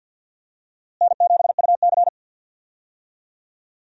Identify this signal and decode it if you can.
Morse code, heard as a keyed tone, about 52 wpm, 700 Hz. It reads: N7UC
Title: USA Call Signs - 4 Characters - Repeated 2x - 50wpm